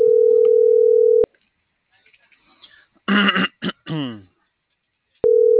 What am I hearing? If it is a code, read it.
Ringback tone (North America), 440 + 480 Hz.